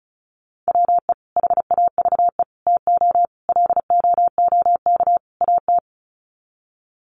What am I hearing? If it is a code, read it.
Morse code, heard as a keyed tone, about 34 wpm, 700 Hz. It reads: WE HAVE TO LOOK AT
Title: Sets of 5 Words Encore2 35wpm